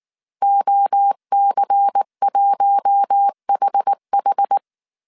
Morse code, heard as a keyed tone, about 19 wpm, 800 Hz. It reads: OC1HH
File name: CW / DET / 1 6 1500C S A 18wpm 2255